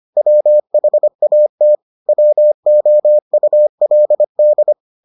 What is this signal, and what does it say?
Morse code, heard as a keyed tone, about 25 wpm, 600 Hz. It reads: WHAT WOULD